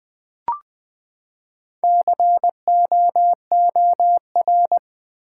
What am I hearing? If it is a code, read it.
Morse code, heard as a keyed tone, about 20 wpm, 700 Hz. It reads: COOR
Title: Build A Word - 4 Letters At A Time - 11-Letter Words - 20wpm